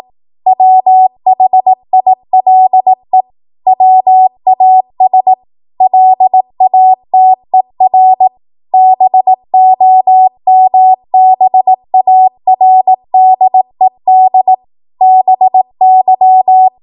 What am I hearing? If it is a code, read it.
Morse code, heard as a keyed tone, about 18 wpm, 750 Hz. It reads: WHILE WAS LATER BOMBARDED BY
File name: Morse / 18WPM